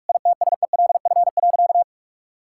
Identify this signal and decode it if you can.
Morse code, heard as a keyed tone, about 46 wpm, 700 Hz. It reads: ITSELF.
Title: Sentences from Top 1000 Words - Review Part 5 - 45wpm